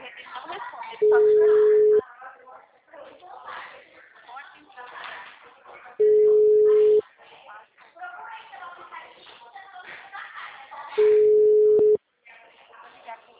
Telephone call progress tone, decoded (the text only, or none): ringback tone